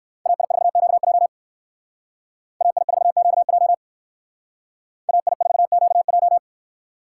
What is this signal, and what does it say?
Morse code, heard as a keyed tone, about 58 wpm, 700 Hz. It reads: AI4QJ AI4QJ AI4QJ